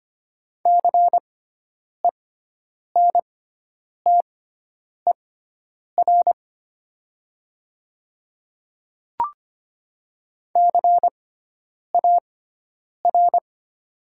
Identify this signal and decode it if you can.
Morse code, heard as a keyed tone, about 25 wpm, spaced as for 9 wpm, 700 Hz. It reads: CENTER CAR
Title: 6x Character Spacing - Top 500 Words - 25wpm